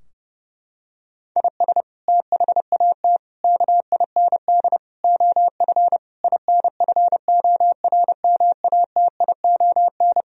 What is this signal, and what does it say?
Morse code, heard as a keyed tone, about 32 wpm, 700 Hz. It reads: IS THAT KIND OF INFORMATION
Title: Sentences from Top 1000 Words - Review Part 3 - 30wpm